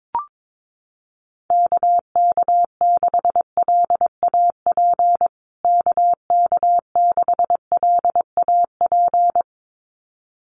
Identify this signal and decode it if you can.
Morse code, heard as a keyed tone, about 22 wpm, 700 Hz. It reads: KK6LAP KK6LAP
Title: USA Call Signs Repeated 2x 22wpm